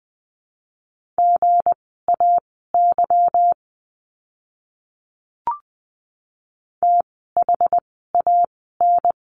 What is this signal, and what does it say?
Morse code, heard as a keyed tone, about 20 wpm, spaced as for 15 wpm, 700 Hz. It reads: GAY THAN
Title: 160 Lesson 26f - Letter g - Words 20wpm